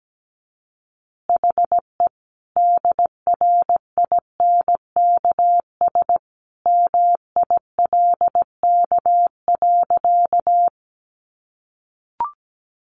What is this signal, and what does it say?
Morse code, heard as a keyed tone, about 17 wpm, 700 Hz. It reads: HE DRINKS MILK.
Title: Easy Phrases 17wpm